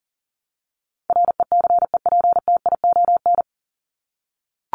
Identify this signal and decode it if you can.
Morse code, heard as a keyed tone, about 40 wpm, 700 Hz. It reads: RECEPTION
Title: Top 1751-2000 Words 40wpm